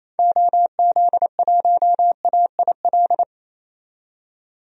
Morse code, heard as a keyed tone, about 27 wpm, 700 Hz. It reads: OZ1AIL
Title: DXCC-100 Call Signs - 6 Characters - 28wpm